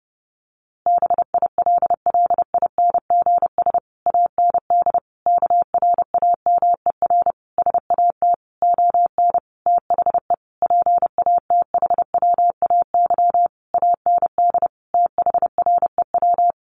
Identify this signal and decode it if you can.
Morse code, heard as a keyed tone, about 30 wpm, 700 Hz. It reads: BILLINGS AND KRAMER SAT ON THE PATHWAY AND THREW
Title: A Sound Of Thunder 30wpm